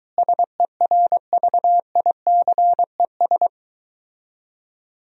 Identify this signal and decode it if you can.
Morse code, heard as a keyed tone, about 23 wpm, 700 Hz. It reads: SERVICES